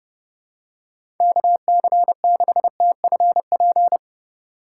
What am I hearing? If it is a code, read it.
Morse code, heard as a keyed tone, about 31 wpm, 700 Hz. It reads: KC6TFP
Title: USA Call Signs - 6 Characters - 3x Character Spacing - 30wpm